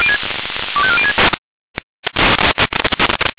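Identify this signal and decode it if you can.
Five-tone selective call (ZVEI), heard as quick runs of five tones, tones of about 70 ms. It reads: 25406
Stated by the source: ZVEI five-tone selective call 25405 received via SDR.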